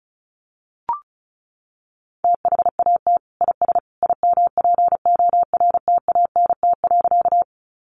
Morse code, heard as a keyed tone, about 34 wpm, 700 Hz. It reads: THAT IS IMPORTANT.
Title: Sentences from Top 600 Words 35wpm